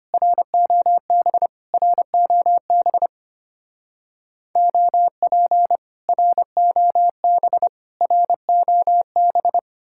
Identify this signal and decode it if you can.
Morse code, heard as a keyed tone, about 30 wpm, 700 Hz. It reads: ROB ROB OP ROB ROB
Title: Speed-Racing Common QSO Phrases - 25wpm